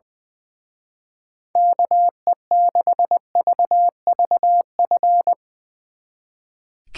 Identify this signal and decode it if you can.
Morse code, heard as a keyed tone, about 21 wpm, 700 Hz. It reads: KE6VVF
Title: Speed Racing - USA 2x3 Call Signs 20wpm